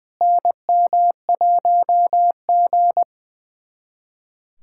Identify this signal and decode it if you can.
Morse code, heard as a keyed tone, about 21 wpm, 700 Hz. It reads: NM1G
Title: USA Callsign DE Callsign - 4 Characters - 20wpm